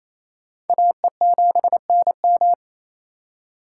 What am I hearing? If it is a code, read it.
Morse code, heard as a keyed tone, about 37 wpm, 700 Hz. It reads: AE7NM